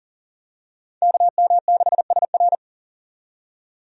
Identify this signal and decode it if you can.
Morse code, heard as a keyed tone, about 40 wpm, 700 Hz. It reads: KM6SR
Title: USA Call Signs - 5 Characters - 40wpm